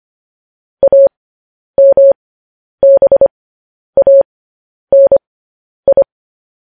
Morse code, heard as a keyed tone, about 25 wpm, spaced as for 10 wpm, 550 Hz. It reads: AMBANI
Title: News Headlines at 10